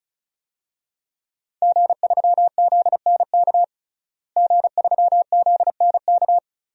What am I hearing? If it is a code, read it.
Morse code, heard as a keyed tone, about 36 wpm, 700 Hz. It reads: G3ZNK G3ZNK
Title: DXCC-100 Call Signs - 5 Characters - Repeated 3x - 35wpm